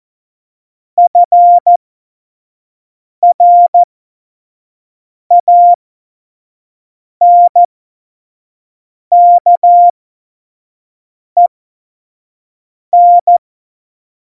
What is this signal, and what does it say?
Morse code, heard as a keyed tone, about 14 wpm, spaced as for 5 wpm, 700 Hz. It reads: FRANKEN
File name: Morsekurs / Witze / Witz 22. Dezember 2020, 25 F70